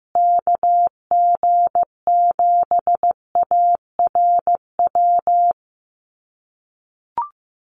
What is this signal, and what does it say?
Morse code, heard as a keyed tone, about 15 wpm, 700 Hz. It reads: KG7ARW
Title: USA Call Signs Repeated 3x 15wpm